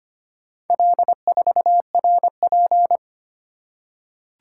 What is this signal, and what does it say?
Morse code, heard as a keyed tone, about 25 wpm, 700 Hz. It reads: L4RP
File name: practice / 25wpm / Four Letter-Number Combinations - 25wpm